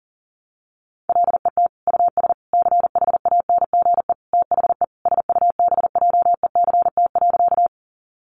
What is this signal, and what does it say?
Morse code, heard as a keyed tone, about 40 wpm, 700 Hz. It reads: LET US CHANGE THE SUBJECT.
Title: Sentences from Top 800 Words 40wpm